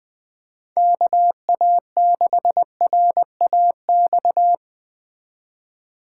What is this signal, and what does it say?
Morse code, heard as a keyed tone, about 21 wpm, 700 Hz. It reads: KA6RAX